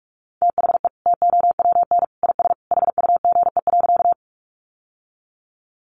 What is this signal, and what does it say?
Morse code, heard as a keyed tone, about 44 wpm, 700 Hz. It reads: THE TOWN IS HUGE.
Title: Sentences from Top 1100 Words 45wpm